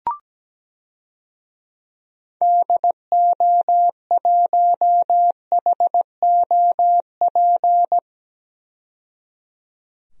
Morse code, heard as a keyed tone, about 17 wpm, 700 Hz. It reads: DO1HOP
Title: DXCC-100 Call Signs - 6 Characters - 17wpm